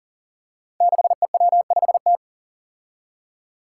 Morse code, heard as a keyed tone, about 40 wpm, 700 Hz. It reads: 6EW5T
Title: Five Challenging Letter-Number Combinations 40wpm